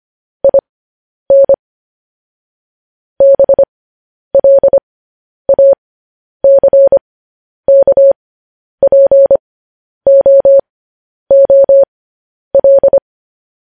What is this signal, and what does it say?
Morse code, heard as a keyed tone, about 25 wpm, spaced as for 10 wpm, 550 Hz. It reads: IN BLACKPOOL
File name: podcast / News headlines-1x-25x10